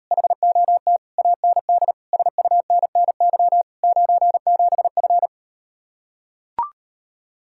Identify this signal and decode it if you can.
Morse code, heard as a keyed tone, about 38 wpm, 700 Hz. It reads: HOT AND SUNNY 97F